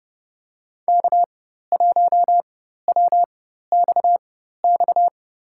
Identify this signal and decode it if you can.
Morse code, heard as a keyed tone, about 30 wpm, spaced as for 14 wpm, 700 Hz. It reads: K1WXX
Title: USA Call Signs - 4x Character Spacing - 30wpm